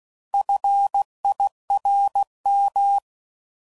Morse code, heard as a keyed tone, about 16 wpm, 800 Hz. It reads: FIRM